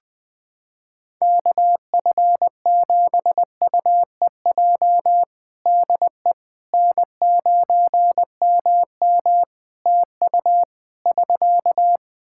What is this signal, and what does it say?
Morse code, heard as a keyed tone, about 20 wpm, 700 Hz. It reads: KF7UEJ DE N9MM TU <SK>
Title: Practice QSO - Exchange 4 - Ending - 20wpm